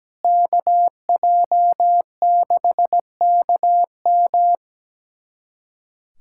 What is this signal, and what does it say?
Morse code, heard as a keyed tone, about 17 wpm, 700 Hz. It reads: KJ6KM
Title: USA Callsign DE Callsign - 5 Characters - 17wpm